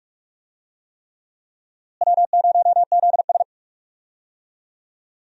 Morse code, heard as a keyed tone, about 41 wpm, 700 Hz. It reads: W0ZS
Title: USA Call Signs - 4 Characters - Repeated 3x - 45wpm